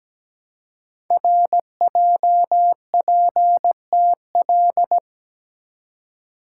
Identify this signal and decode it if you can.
Morse code, heard as a keyed tone, about 17 wpm, 700 Hz. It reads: RJPTL
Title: Code Groups - 5 Letters - 17wpm